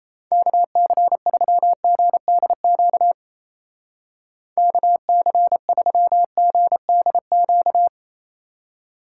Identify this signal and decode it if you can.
Morse code, heard as a keyed tone, about 35 wpm, 700 Hz. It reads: KC3GDQ KC3GDQ